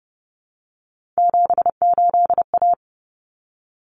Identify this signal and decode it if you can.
Morse code, heard as a keyed tone, about 30 wpm, 700 Hz. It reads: 78A
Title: Common Cut Numbers - 3-Digit 30wpm